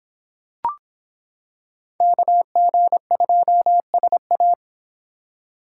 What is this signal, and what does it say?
Morse code, heard as a keyed tone, about 26 wpm, 700 Hz. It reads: KG2SA